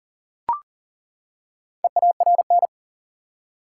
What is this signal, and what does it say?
Morse code, heard as a keyed tone, about 40 wpm, 700 Hz. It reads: EARN
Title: Four Letter Words - 40wpm